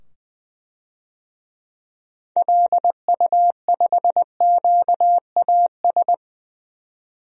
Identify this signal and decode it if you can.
Morse code, heard as a keyed tone, about 20 wpm, 700 Hz. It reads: LU5QAS
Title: DXCC-100 Call Signs - 6 Characters - 20wpm